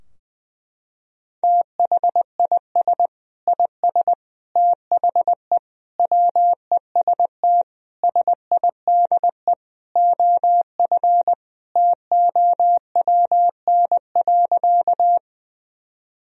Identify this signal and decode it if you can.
Morse code, heard as a keyed tone, about 21 wpm, 700 Hz. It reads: THIS IS THE WEST SIDE OF TOWN.
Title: Sentences from Top 1000 Words - Review Part 2 - 20wpm